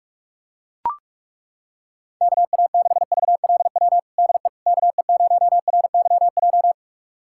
Morse code, heard as a keyed tone, about 42 wpm, 700 Hz. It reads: KA6VLW DE KE0RYJ
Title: USA Callsign DE Callsign - 6 Characters - 45wpm